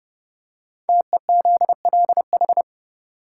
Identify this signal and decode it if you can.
Morse code, heard as a keyed tone, about 30 wpm, 700 Hz. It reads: TEZLH
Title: Code Groups - 5 Letters - 30wpm